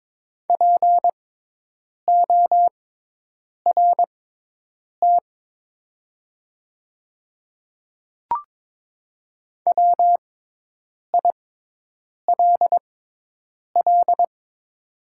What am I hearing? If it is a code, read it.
Morse code, heard as a keyed tone, about 22 wpm, spaced as for 8 wpm, 700 Hz. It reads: PORT WILL
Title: 6x Character Spacing - Top 300 Words - 22wpm